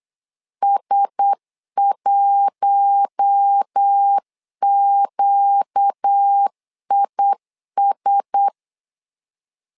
Morse code, heard as a keyed tone, about 8 wpm, 800 Hz. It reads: S1QIS
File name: LEZIONI / 4 Q1S 8wpm 2710